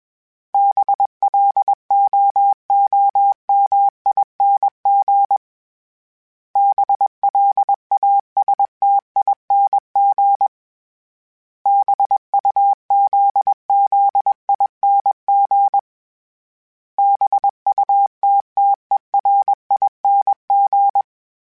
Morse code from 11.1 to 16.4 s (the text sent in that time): BUZZING